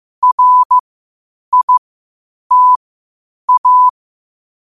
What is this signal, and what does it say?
Morse code, heard as a keyed tone, about 15 wpm, spaced as for 8 wpm, 1000 Hz. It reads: RITA
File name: qotdfiles / 8wpm